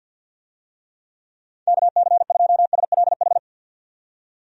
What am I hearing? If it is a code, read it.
Morse code, heard as a keyed tone, about 52 wpm, 700 Hz. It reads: KC2SLH
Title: USA Call Signs - 6 Characters - 50wpm